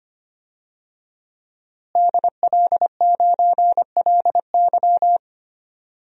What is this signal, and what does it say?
Morse code, heard as a keyed tone, about 25 wpm, 700 Hz. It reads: DL9LY
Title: DXCC-100 Call Signs - 5 Characters - Repeated 2x - 25wpm